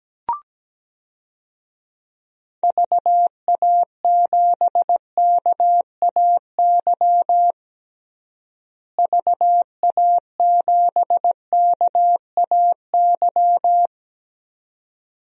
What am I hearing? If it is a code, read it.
Morse code, heard as a keyed tone, about 17 wpm, 700 Hz. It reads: VA7KAY VA7KAY
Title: DXCC-100 Call Signs - 6 Characters - Repeated 2x - 17wpm